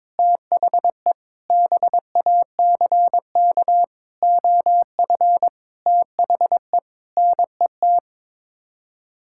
Morse code, heard as a keyed tone, about 22 wpm, 700 Hz. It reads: THE BACK OF THE NET
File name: practice / 22wpm / Sets of 8 Words - 22wpm